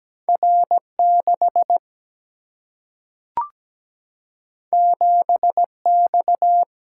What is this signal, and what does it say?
Morse code, heard as a keyed tone, about 17 wpm, 700 Hz. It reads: R6 7X
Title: Single-Letter Single-Number 17wpm